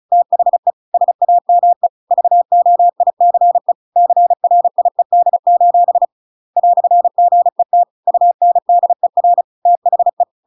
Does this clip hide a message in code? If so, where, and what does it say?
Yes: Morse code from 0.1 to 10.2 s, heard as a keyed tone, about 34 wpm, 700 Hz. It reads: THE SAME VOICE CRIED: "GET UNDER THE